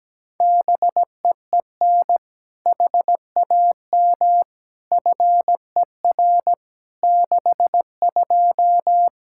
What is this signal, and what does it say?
Morse code, heard as a keyed tone, about 17 wpm, 700 Hz. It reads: BEEN HAM FER 62
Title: K1USN SST QSO Practice 17wpm